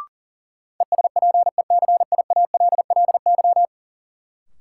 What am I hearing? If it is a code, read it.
Morse code, heard as a keyed tone, about 40 wpm, 700 Hz. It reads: ESPECIALLY
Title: Sentences from Top 700 Words - New Word Drill - 40wpm